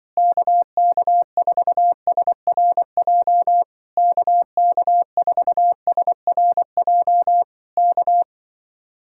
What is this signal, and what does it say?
Morse code, heard as a keyed tone, about 24 wpm, 700 Hz. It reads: KK4SRJ KK4SRJ K
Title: Speed-Racing Common QSO Phrases - 20wpm